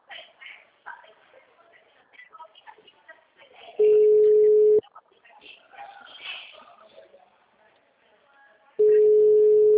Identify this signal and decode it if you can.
Ringback tone (Europe, 425 Hz), with speech.
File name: out- -2004- - - .87854